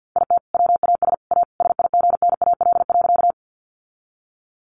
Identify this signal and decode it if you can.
Morse code, heard as a keyed tone, about 50 wpm, 700 Hz. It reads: IT WAS A SIGNAL.
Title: Sentences from Top 1400 Words 50wpm